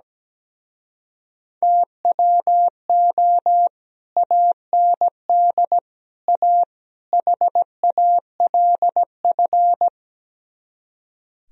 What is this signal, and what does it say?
Morse code, heard as a keyed tone, about 17 wpm, 700 Hz. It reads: TWO AND A HALF
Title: Speed-Racing - Sets of 4 Words 17wpm